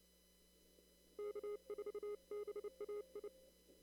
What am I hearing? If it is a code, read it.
Morse code, heard as a keyed tone, about 28 wpm, 410 Hz. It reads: K4BAI
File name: CW / kt5j wpx cw-op-k5pi